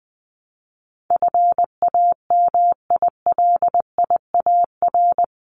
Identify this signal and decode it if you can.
Morse code, heard as a keyed tone, about 20 wpm, 700 Hz. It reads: FAMILIAR
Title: Top 751-1000 Words 20wpm